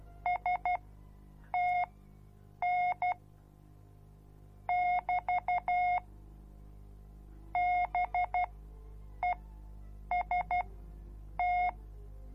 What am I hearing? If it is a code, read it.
Morse code, heard as a keyed tone, about 12 wpm, spaced as for 7 wpm, 720 Hz. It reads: STN <BT> BEST